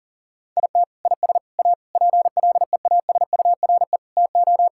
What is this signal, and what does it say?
Morse code, heard as a keyed tone, about 37 wpm, 700 Hz. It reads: IT IS A PLEASURE TO
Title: Sentences from Top 1600 Words 40wpm